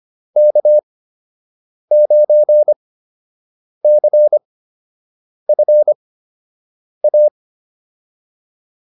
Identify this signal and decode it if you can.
Morse code, heard as a keyed tone, about 25 wpm, spaced as for 7 wpm, 600 Hz. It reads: K9CFA